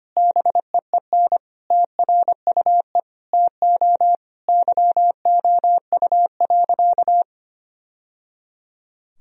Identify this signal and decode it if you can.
Morse code, heard as a keyed tone, about 25 wpm, 700 Hz. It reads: BEEN TRUE TO YOU.
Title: Sentences from Top 1000 Words - Review Part 4 - 25wpm